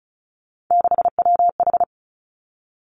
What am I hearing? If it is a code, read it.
Morse code, heard as a keyed tone, about 36 wpm, 700 Hz. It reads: 6WH
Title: Three Challenging Letter-Number Combinations 35wpm